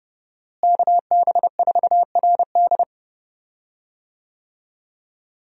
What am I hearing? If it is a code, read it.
Morse code, heard as a keyed tone, about 30 wpm, 700 Hz. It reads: KB4RD
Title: USA Call Signs - 5 Characters - Repeated 2x - 30wpm